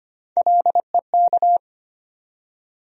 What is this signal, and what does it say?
Morse code, heard as a keyed tone, about 25 wpm, 700 Hz. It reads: LEK
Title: USA Call Sign Postfixes - 3 Letters 25wpm